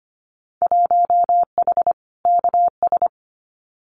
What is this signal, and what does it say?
Morse code, heard as a keyed tone, about 25 wpm, 700 Hz. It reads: 1H KS
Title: Winter Field Day Exchanges Encore 25wpm